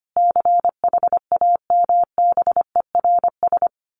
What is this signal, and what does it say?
Morse code, heard as a keyed tone, about 25 wpm, 700 Hz. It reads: CHAMBERS
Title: A Sound Of Thunder 25wpm